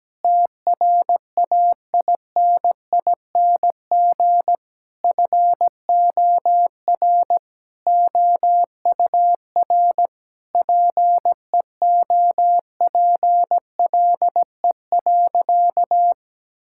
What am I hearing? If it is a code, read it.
Morse code, heard as a keyed tone, about 17 wpm, 700 Hz. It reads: TRAINING FOR OUR PEOPLE.
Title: Sentences from Top 1000 Words - Review Part 5 - 17wpm